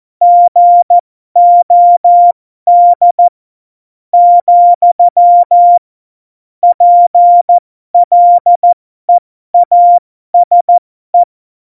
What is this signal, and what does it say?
Morse code, heard as a keyed tone, about 14 wpm, spaced as for 12 wpm, 700 Hz. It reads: GOD , PLEASE
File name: Morsekurs / Witze